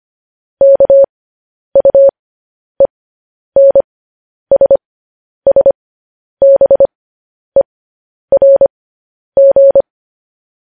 Morse code, heard as a keyed tone, about 25 wpm, spaced as for 10 wpm, 550 Hz. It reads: KUENSSBERG
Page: news headlines in morse code at 10 wpm